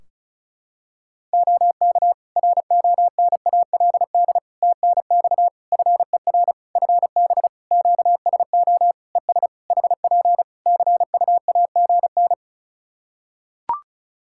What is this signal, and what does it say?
Morse code, heard as a keyed tone, about 34 wpm, 700 Hz. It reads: OK RONALD TNX FER FB QSO ES HP CUAGN
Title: Practice QSO - Exchange 4 - Ending - 35wpm